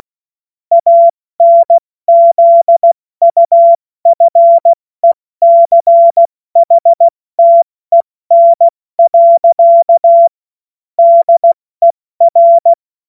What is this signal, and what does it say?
Morse code, heard as a keyed tone, about 16 wpm, 680 Hz. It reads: ANZUFECHTEN. DER
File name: Morsekurs / Witze / Witz 25. Oktober 2022 70 F80